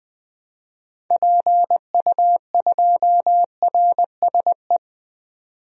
Morse code, heard as a keyed tone, about 20 wpm, 700 Hz. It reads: PU2RSE
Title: DXCC-100 Call Signs - 6 Characters - 20wpm